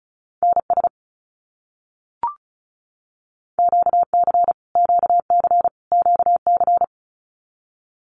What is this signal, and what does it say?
Morse code, heard as a keyed tone, about 34 wpm, 700 Hz. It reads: NS QC QC QC
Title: Canadian Provinces and Territories - Repeated 3x - 35wpm